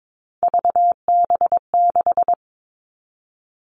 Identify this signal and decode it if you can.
Morse code, heard as a keyed tone, about 22 wpm, 700 Hz. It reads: VB6
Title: All Cut Numbers - 3-Digit 22wpm